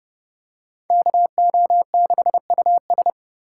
Morse code, heard as a keyed tone, about 30 wpm, 700 Hz. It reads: KO6US
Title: USA Call Signs - 5 Characters - 2x Character Spacing - 30wpm